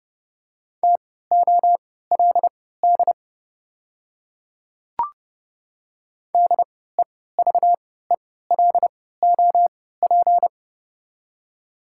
Morse code, heard as a keyed tone, about 31 wpm, spaced as for 17 wpm, 700 Hz. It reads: TOLD DEVELOP
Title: 3x Character Spacing - Top 500 Words - 30wpm